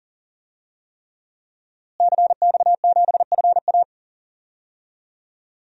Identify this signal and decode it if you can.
Morse code, heard as a keyed tone, about 40 wpm, 700 Hz. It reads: CX7FA
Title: DXCC-100 Call Signs - 5 Characters - Repeated 3x - 40wpm